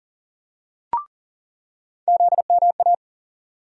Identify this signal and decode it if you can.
Morse code, heard as a keyed tone, about 40 wpm, 700 Hz. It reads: ZMA
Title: USA Call Sign Postfixes - 3 Letters 40wpm